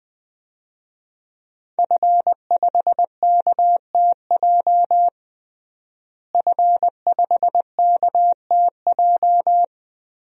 Morse code, heard as a keyed tone, about 20 wpm, 700 Hz. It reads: F5KTJ F5KTJ
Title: DXCC-100 Call Signs - 5 Characters - Repeated 3x - 20wpm